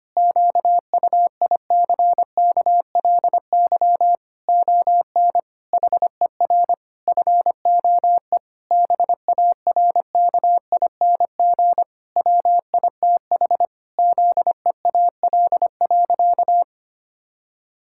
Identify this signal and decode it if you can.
Morse code, heard as a keyed tone, about 25 wpm, 700 Hz. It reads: QUICKLY ON HER FOE BARKING WITH ZEAL.
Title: Pangram Challenge v2 - 25wpm